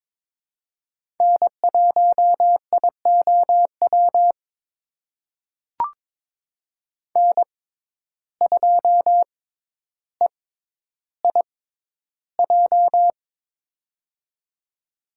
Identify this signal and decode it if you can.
Morse code, heard as a keyed tone, about 22 wpm, spaced as for 11 wpm, 700 Hz. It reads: N1IOW N2EIJ
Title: USA Call Signs - 5 Characters - 6x Character Spacing - 22wpm